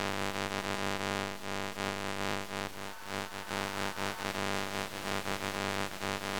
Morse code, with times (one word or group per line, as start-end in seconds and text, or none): none